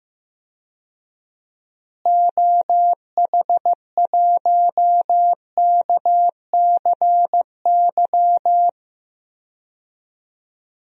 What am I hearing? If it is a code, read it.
Morse code, heard as a keyed tone, about 16 wpm, 700 Hz. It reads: OH1KCY
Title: DXCC-100 Call Signs - 6 Characters - Repeated 3x - 15wpm